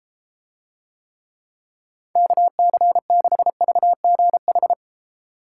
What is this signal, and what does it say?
Morse code, heard as a keyed tone, about 32 wpm, 700 Hz. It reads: KC6VGH